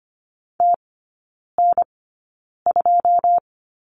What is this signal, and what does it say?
Morse code, heard as a keyed tone, about 25 wpm, spaced as for 9 wpm, 690 Hz. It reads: TN2
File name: Lesson 14e - Number 2 - Three Characters 25wpm